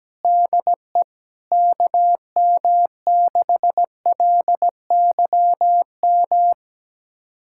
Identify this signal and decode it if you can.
Morse code, heard as a keyed tone, about 17 wpm, 700 Hz. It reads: DE KM6LYM
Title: USA Callsign DE Callsign 17wpm